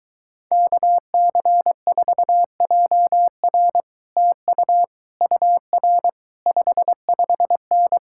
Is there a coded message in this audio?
Yes: Morse code from 0.5 to 8.0 s, heard as a keyed tone, about 23 wpm, 700 Hz. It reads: KC4JR TU UR 55N